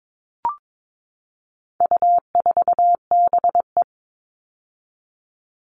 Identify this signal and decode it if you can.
Morse code, heard as a keyed tone, about 22 wpm, 700 Hz. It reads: U4BE
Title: All Cut Numbers - 4-Digit 22wpm